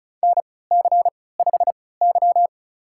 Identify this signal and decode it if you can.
Morse code, heard as a keyed tone, about 34 wpm, spaced as for 20 wpm, 700 Hz. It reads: NC5Y
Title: USA Call Signs - 4 Characters - 3x Character Spacing - 35wpm